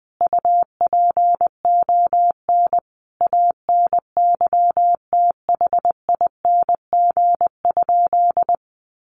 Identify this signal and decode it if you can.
Morse code, heard as a keyed tone, about 20 wpm, 700 Hz. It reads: UPON ANYTHING?
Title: Sentences from Top 1200 Words 20wpm